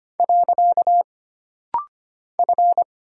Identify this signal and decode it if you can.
Morse code, heard as a keyed tone, about 25 wpm, 700 Hz. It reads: . F
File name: Single Character (Letter:Number:Punctuation:Prosign) - Rapid-Fire - 25wpm